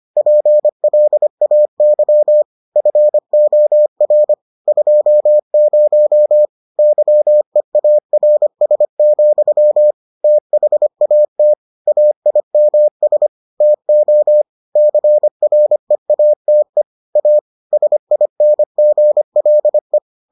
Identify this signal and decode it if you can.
Morse code, heard as a keyed tone, about 25 wpm, 600 Hz. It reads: PLAY FOR 20 YEARS, THAT AIMS TO CREATE A SINGLE